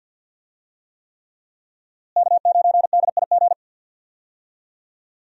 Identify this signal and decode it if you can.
Morse code, heard as a keyed tone, about 53 wpm, 700 Hz. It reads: K9DIG
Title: USA Call Signs - 5 Characters - Repeated 3x - 50wpm